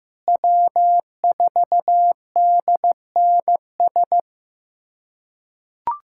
Morse code, heard as a keyed tone, about 16 wpm, 700 Hz. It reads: W4DNS
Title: USA Callsign DE Callsign - 5 Characters - 15wpm